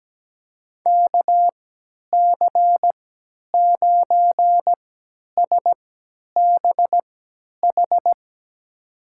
Morse code, heard as a keyed tone, about 17 wpm, spaced as for 10 wpm, 700 Hz. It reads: KC9SBH
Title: USA Call Signs - 6 Characters - 3x Character Spacing - 17wpm